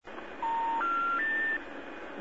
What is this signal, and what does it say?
Special information tone, with noise.
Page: modern-north-american-telephone-sounds/att-general-error-recordings